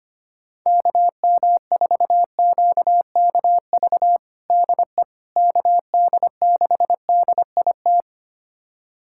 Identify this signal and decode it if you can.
Morse code, heard as a keyed tone, about 25 wpm, 700 Hz. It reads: KM4QKV DE KD6DIT
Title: USA Callsign DE Callsign - 6 Characters - 25wpm